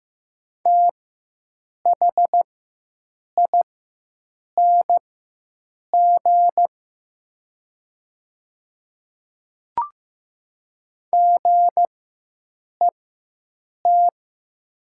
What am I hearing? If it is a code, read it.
Morse code, heard as a keyed tone, about 15 wpm, spaced as for 7 wpm, 700 Hz. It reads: THING GET
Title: 4x Character Spacing - Top 200 Words - 15wpm